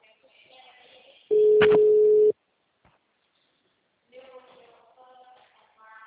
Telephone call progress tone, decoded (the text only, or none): ringback tone